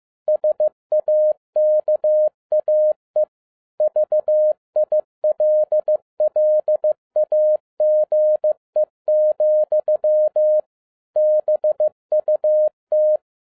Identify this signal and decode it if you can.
Morse code, heard as a keyed tone, about 15 wpm, 610 Hz. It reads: SAKAE VILLAGE, BUT
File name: 2011-10-mp3 / n02-FIELD DAY CONTEST QRV FROM SAKAE VILLAGE 15wpm